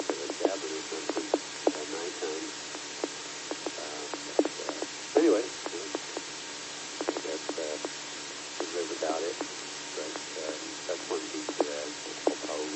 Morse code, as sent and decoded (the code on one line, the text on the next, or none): none